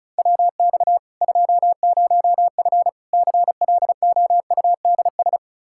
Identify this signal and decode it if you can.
Morse code, heard as a keyed tone, about 34 wpm, 700 Hz. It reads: WX 20F CLOUDS
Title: Practice QSO - Exchange 2 - Info - 35wpm